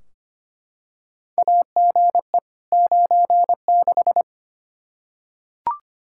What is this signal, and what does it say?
Morse code, heard as a keyed tone, about 25 wpm, 700 Hz. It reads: AGE 96
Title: Practice QSO - Exchange 3 - More Info - 25wpm